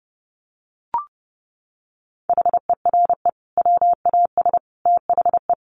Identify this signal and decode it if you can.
Morse code, heard as a keyed tone, about 30 wpm, 700 Hz. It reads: HERE WAS THE
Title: Sentences from Top 1500 Words 30wpm